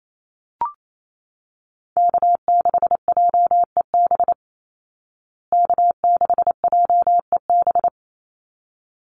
Morse code, heard as a keyed tone, about 28 wpm, 690 Hz. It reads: K6JEB K6JEB
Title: Call Signs Repeated 3x - 28wpm